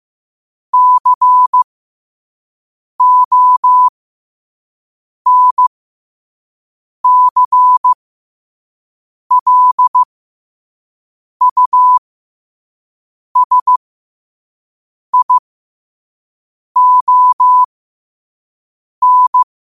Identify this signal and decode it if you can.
Morse code, heard as a keyed tone, about 15 wpm, spaced as for 5 wpm, 1000 Hz. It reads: CONCLUSION
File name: 2026-02-15-5wpm